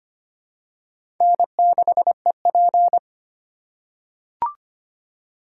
Morse code, heard as a keyed tone, about 25 wpm, 700 Hz. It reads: N6EP
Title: USA Call Signs - 4 Characters - 2x Character Spacing - 25wpm